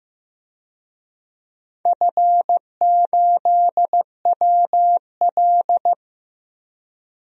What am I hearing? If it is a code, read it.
Morse code, heard as a keyed tone, about 16 wpm, 700 Hz. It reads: F8WL
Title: DXCC-100 Call Signs - 4 Characters - 15wpm